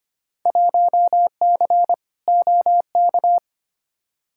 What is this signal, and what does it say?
Morse code, heard as a keyed tone, about 25 wpm, 700 Hz. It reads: 1C OK
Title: Field Day Exchanges Encore - 25wpm